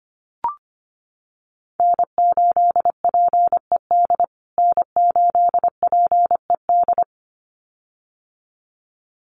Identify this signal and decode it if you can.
Morse code, heard as a keyed tone, about 25 wpm, 700 Hz. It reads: N8PED N8PED
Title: USA Call Signs Repeated 2x 25wpm